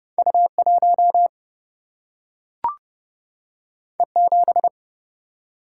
Morse code, heard as a keyed tone, about 30 wpm, 700 Hz. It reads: U 1 E 7
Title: Single-Letter Single-Number 30wpm